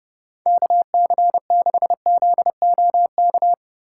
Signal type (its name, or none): Morse code